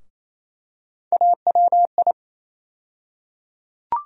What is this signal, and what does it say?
Morse code, heard as a keyed tone, about 28 wpm, 700 Hz. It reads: AWI
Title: USA Call Sign Postfixes - 3 Letters 28wpm